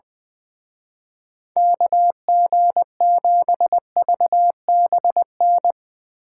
Morse code, heard as a keyed tone, about 22 wpm, 700 Hz. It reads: KG7VBN
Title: Speed Racing - USA 2x3 Call Signs 20wpm